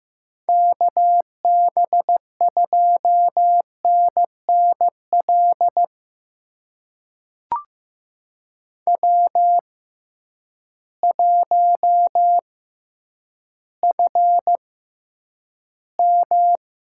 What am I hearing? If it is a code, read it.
Morse code, heard as a keyed tone, about 15 wpm, spaced as for 9 wpm, 700 Hz. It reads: KB2NNL W1FM
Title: USA Call Signs - 6x Character Spacing - 15wpm